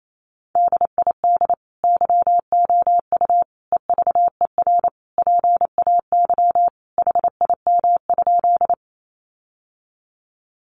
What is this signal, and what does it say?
Morse code, heard as a keyed tone, about 28 wpm, 700 Hz. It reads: DID YOU EVER PAY HIM?
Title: Sentences from Top 700 Words 28wpm